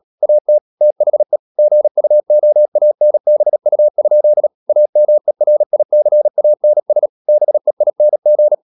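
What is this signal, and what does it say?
Morse code, heard as a keyed tone, about 37 wpm, 600 Hz. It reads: AT THE GUOANBU? AMERICANS BEING
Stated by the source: Morse practice at 36 wpm, text of Ken Follett's novel Never, from CW med Gnister.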